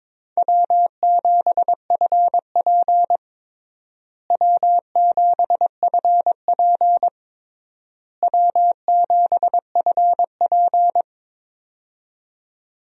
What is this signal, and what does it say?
Morse code, heard as a keyed tone, about 22 wpm, 700 Hz. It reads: W7FP W7FP W7FP
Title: USA Call Signs - 4 Characters - Repeated 3x - 22wpm